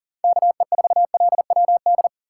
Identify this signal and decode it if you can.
Morse code, heard as a keyed tone, about 40 wpm, 700 Hz. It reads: KE4LWD
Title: USA Call Signs - 6 Characters - 4x Character Spacing - 40wpm